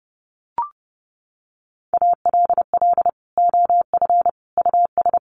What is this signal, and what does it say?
Morse code, heard as a keyed tone, about 30 wpm, 700 Hz. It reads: ALL OF US
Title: Sets of 3 Words 30wpm